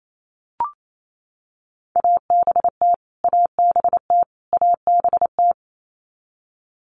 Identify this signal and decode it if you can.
Morse code, heard as a keyed tone, about 28 wpm, 700 Hz. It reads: ABT ABT ABT
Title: Abbreviations Repeated 3x - 28wpm